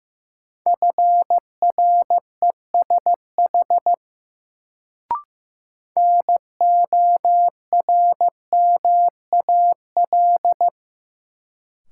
Morse code, heard as a keyed tone, about 15 wpm, 700 Hz. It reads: FRESH NORMAL
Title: Sentences from Top 1200 Words - New Word Drill Alt - 15wpm